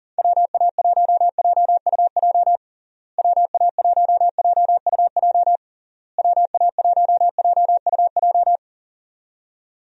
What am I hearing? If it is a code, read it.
Morse code, heard as a keyed tone, about 40 wpm, 700 Hz. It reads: WA1JUJ WA1JUJ WA1JUJ
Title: USA Call Signs - 6 Characters - Repeated 3x - 40wpm